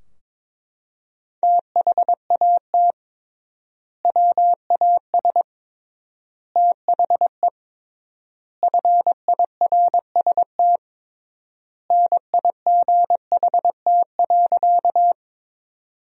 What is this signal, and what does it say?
Morse code, heard as a keyed tone, about 22 wpm, 700 Hz. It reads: THAT WAS THE FIRST NIGHT.
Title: 3x Word-Spacing - Sentences from Top 300 Words - 22wpm